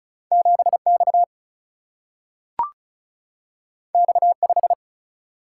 Morse code, heard as a keyed tone, about 34 wpm, 700 Hz. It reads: 7X X5
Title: Single-Letter Single-Number 35wpm